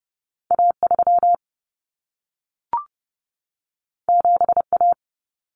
Morse code, heard as a keyed tone, about 30 wpm, 700 Hz. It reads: A3 7A
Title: Common Cut Numbers - 2-Digit 30wpm